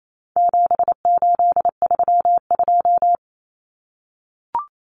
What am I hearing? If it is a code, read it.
Morse code, heard as a keyed tone, about 28 wpm, 700 Hz. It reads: 7832
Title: Common Cut Numbers - 4-Digit 28wpm